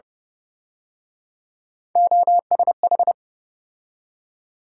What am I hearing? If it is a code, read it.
Morse code, heard as a keyed tone, about 30 wpm, 700 Hz. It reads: OSH
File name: Speed-Racing - Three Letter-Number Combinations - 25wpm